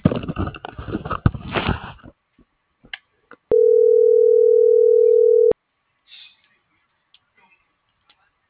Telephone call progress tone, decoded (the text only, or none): ringback tone